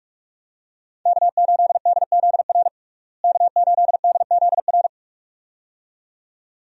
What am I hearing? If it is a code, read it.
Morse code, heard as a keyed tone, about 47 wpm, 700 Hz. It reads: K8DZR K8DZR
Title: USA Call Signs - 5 Characters - Repeated 2x - 45wpm